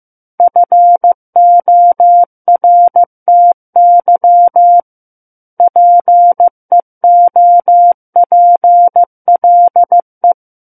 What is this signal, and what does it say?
Morse code, heard as a keyed tone, about 15 wpm, 700 Hz. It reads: FORTY PEOPLE
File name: Book One / 06-Chapter 6 - 15 WPM